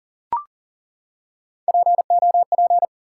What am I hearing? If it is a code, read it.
Morse code, heard as a keyed tone, about 40 wpm, 700 Hz. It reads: POP
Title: Three Letter Words - 40wpm